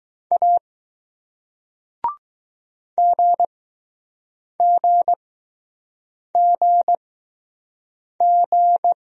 Morse code, heard as a keyed tone, about 20 wpm, spaced as for 6 wpm, 700 Hz. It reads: A GGGG